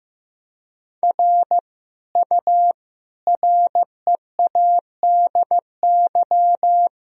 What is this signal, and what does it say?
Morse code, heard as a keyed tone, about 16 wpm, 700 Hz. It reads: R U READY
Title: Sentences from Top 400 Words and Abbreviations 15wpm